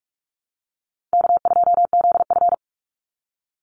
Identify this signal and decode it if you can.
Morse code, heard as a keyed tone, about 44 wpm, 700 Hz. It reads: K2ZF
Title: Easy Call Signs Repeated 3x - 45wpm